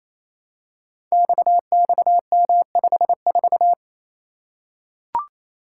Morse code, heard as a keyed tone, about 28 wpm, 700 Hz. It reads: XXM54
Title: Code Groups - 5 Letter:Numbers - 28wpm